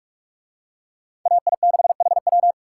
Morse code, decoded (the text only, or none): AI6HW